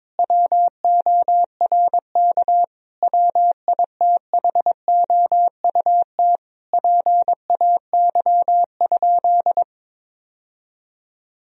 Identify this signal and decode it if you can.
Morse code, heard as a keyed tone, about 22 wpm, 700 Hz. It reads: WORK WITHOUT PAY?
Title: Sentences from Top 1000 Words - Review Part 3 - 22wpm